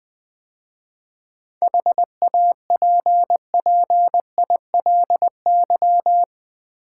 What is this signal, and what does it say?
Morse code, heard as a keyed tone, about 20 wpm, 700 Hz. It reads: HAPPILY
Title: 7 Letter Words - First 3 Letters Given - 20wpm